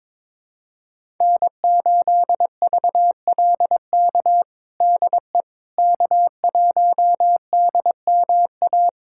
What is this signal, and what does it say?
Morse code, heard as a keyed tone, about 22 wpm, 700 Hz. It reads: N8VLK DE K1DMA
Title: USA Callsign DE Callsign - 5 Characters - 22wpm